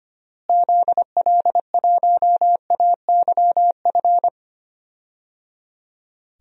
Morse code, heard as a keyed tone, about 25 wpm, 700 Hz. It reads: ZL1AYF
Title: DXCC-100 Call Signs - 6 Characters - Repeated 3x - 25wpm